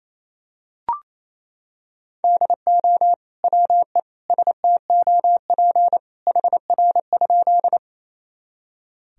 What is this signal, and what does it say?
Morse code, heard as a keyed tone, about 28 wpm, 700 Hz. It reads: DO WE STOP HR?
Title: Sentences from Top 300 Words and Abbreviations 28wpm